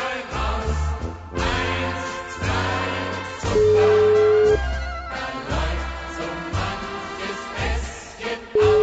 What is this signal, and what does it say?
Ringback tone (Europe, 425 Hz), with noise.